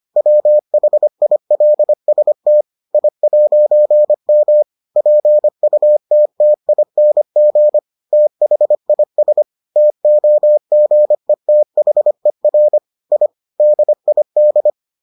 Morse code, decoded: WHILST I'M PUTTING THIS TOGETHER I DID

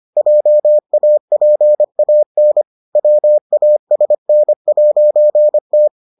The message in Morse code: JAPAN WASN'T

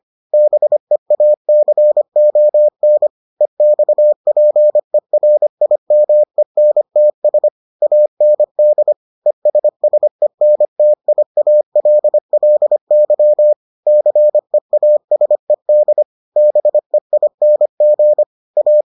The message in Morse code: BEACON EXPERIMENTS AND ESSENTIALLY CEASED BEING A

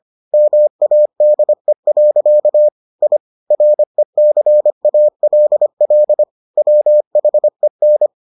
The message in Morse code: MADE. I RECALL WHEN

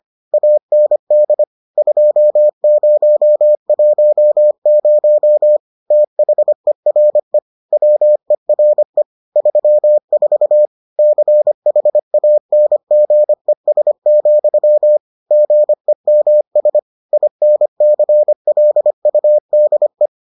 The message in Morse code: AND 2010 THERE WERE 34 CHANGES, GEMS INCLUDE

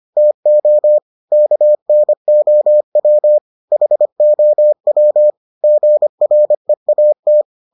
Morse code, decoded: TO KNOW HOW GREAT

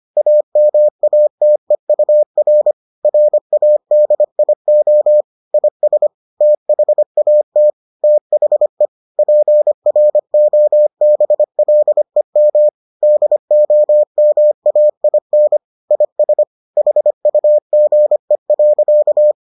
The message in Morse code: AMATEUR RADIO IS THAT THE PROBLEM DOMAIN IS HUGE.

